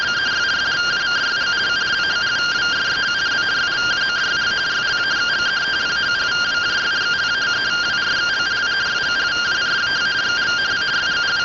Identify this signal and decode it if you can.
Two-tone teleprinter signal, about 170 Hz shift.